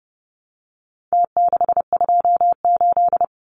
Morse code, heard as a keyed tone, about 30 wpm, 700 Hz. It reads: T628
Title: Numbers - Contest Style - 4-Digit 30wpm